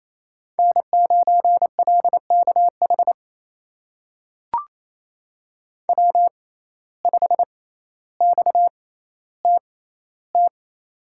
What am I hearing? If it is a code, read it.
Morse code, heard as a keyed tone, about 28 wpm, spaced as for 14 wpm, 700 Hz. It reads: N9LKH W5XTT
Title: USA Call Signs - 5 Characters - 6x Character Spacing - 28wpm